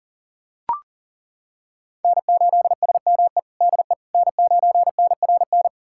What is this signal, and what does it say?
Morse code, heard as a keyed tone, about 40 wpm, 700 Hz. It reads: N8SME DE N9NRN
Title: USA Callsign DE Callsign - 5 Characters - 40wpm